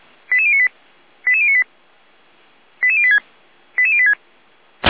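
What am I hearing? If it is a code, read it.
Five-tone selective call (ZVEI), heard as quick runs of five tones, tones of about 70 ms. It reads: 88098 88098 88086 88086
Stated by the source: ZVEI five-tone selective call 88571, received by SDR.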